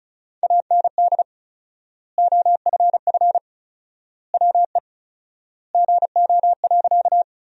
Morse code, heard as a keyed tone, about 36 wpm, 700 Hz. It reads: AND OFF WE GO.
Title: 4x Word-Spacing - Sentences from Top 200 Words - 35wpm